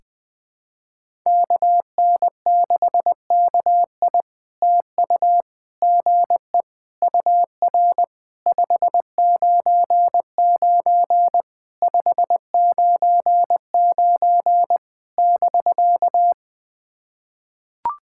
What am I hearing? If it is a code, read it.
Morse code, heard as a keyed tone, about 20 wpm, 700 Hz. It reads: KN6KI TU GE UR 599 599 <BK>
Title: POTA QSO Practice - 20wpm